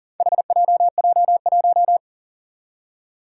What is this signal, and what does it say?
Morse code, heard as a keyed tone, about 40 wpm, 700 Hz. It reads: HJJ1
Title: Four Challenging Letter-Number Combinations 40wpm